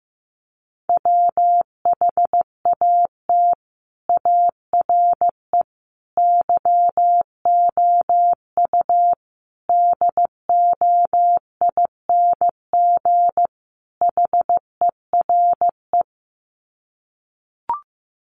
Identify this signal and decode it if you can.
Morse code, heard as a keyed tone, about 15 wpm, 700 Hz. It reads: WHAT ARE YOU DOING HERE
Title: Sets of 5 Words 15wpm